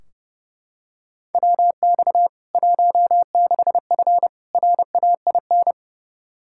Morse code, heard as a keyed tone, about 30 wpm, 700 Hz. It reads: WX 16F RAIN
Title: K1USN SST QSO Practice 30wpm